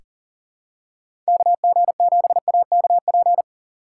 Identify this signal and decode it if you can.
Morse code, heard as a keyed tone, about 40 wpm, 700 Hz. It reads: KG7AKP
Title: USA Call Signs - 6 Characters - 2x Character Spacing - 40wpm